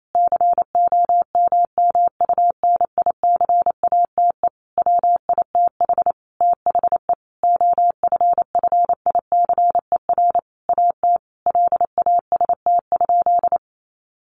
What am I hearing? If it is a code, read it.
Morse code, heard as a keyed tone, about 28 wpm, 700 Hz. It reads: COMMUNICATE WITH THE OFFICER AT LAST?
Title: Sentences from Top 1300 Words 28wpm